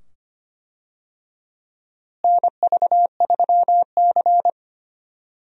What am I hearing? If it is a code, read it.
Morse code, heard as a keyed tone, about 25 wpm, 700 Hz. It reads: NV3C
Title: USA Call Signs - 4 Characters - Repeated 2x - 25wpm